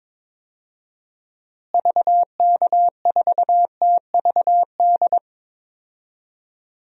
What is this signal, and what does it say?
Morse code, heard as a keyed tone, about 22 wpm, 700 Hz. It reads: VK4TVD
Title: DXCC-100 Call Signs - 6 Characters - Repeated 2x - 22wpm